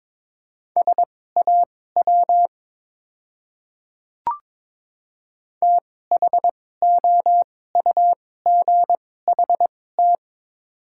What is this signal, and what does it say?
Morse code, heard as a keyed tone, about 22 wpm, spaced as for 16 wpm, 700 Hz. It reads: SAW THOUGHT
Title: 2x Character Spacing - Top 300 Words - 22wpm